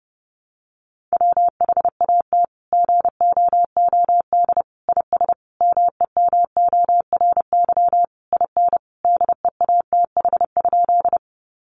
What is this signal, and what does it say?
Morse code, heard as a keyed tone, about 30 wpm, 700 Hz. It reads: WHAT GOOD IS MEMORY IN DEATH?
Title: Sentences from Top 1300 Words 30wpm